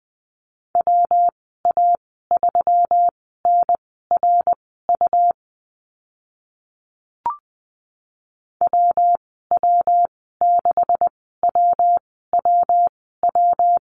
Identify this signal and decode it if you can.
Morse code, heard as a keyed tone, about 20 wpm, spaced as for 14 wpm, 700 Hz. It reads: WA3NRU WW6WWW
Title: 140 Lesson 23g - Letter f - Call Signs 20wpm